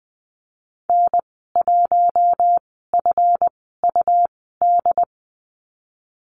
Morse code, heard as a keyed tone, about 20 wpm, spaced as for 15 wpm, 700 Hz. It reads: N1FUD
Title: 154 Lesson 25g - Letter p - Call Signs 20wpm